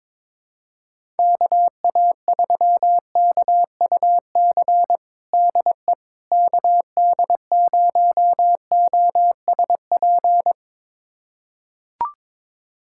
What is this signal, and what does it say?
Morse code, heard as a keyed tone, about 22 wpm, 700 Hz. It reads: KA3KUC DE KD0OSP
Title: USA Callsign DE Callsign 22wpm